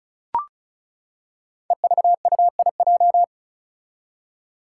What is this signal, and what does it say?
Morse code, heard as a keyed tone, about 36 wpm, 700 Hz. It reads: EVUIJ
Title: Five Challenging Letter-Number Combinations 35wpm